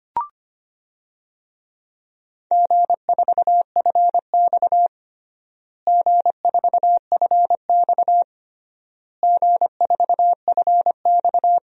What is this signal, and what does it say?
Morse code, heard as a keyed tone, about 25 wpm, 700 Hz. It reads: G4FX G4FX G4FX
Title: DXCC-100 Call Signs - 4 Characters - Repeated 3x - 25wpm